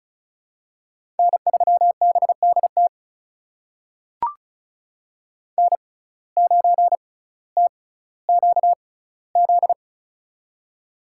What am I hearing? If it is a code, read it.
Morse code, heard as a keyed tone, about 36 wpm, spaced as for 18 wpm, 700 Hz. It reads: N3BDT N9TQZ
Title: USA Call Signs - 5 Characters - 6x Character Spacing - 35wpm